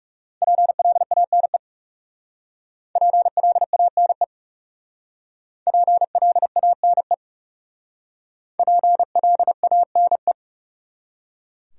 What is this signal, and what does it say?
Morse code, heard as a keyed tone, about 38 wpm, 700 Hz. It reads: PLANE PLANE PLANE PLANE